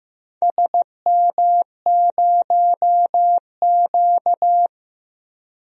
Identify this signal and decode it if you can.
Morse code, heard as a keyed tone, about 15 wpm, 700 Hz. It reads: SM0Q
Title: DXCC-100 Call Signs - 4 Characters - Repeated 2x - 15wpm